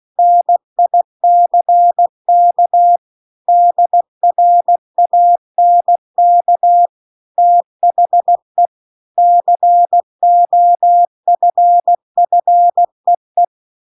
Morse code, heard as a keyed tone, about 16 wpm, 700 Hz. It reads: NICK DRANK THE COFFEE